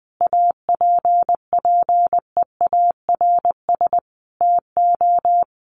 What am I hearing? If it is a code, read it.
Morse code, heard as a keyed tone, about 20 wpm, 700 Hz. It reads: APPEARS TO
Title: Sets of 5 Words Encore2 20wpm